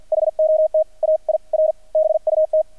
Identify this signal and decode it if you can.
Morse code, heard as a keyed tone, about 50 wpm, spaced as for 42 wpm, 630 Hz. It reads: HOT AIR BUT
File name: archived / CRSnet